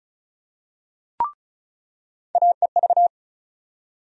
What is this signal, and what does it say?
Morse code, heard as a keyed tone, about 34 wpm, 700 Hz. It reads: AEV
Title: USA Call Sign Postfixes - 3 Letters 35wpm